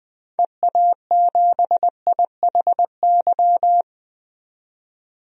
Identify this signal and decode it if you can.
Morse code, heard as a keyed tone, about 20 wpm, 700 Hz. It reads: EA7IHY
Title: DXCC-100 Call Signs - 6 Characters - 20wpm